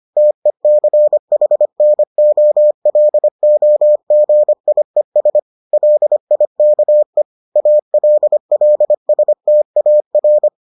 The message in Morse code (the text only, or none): TECHNOLOGIES LIKE ALLSTAR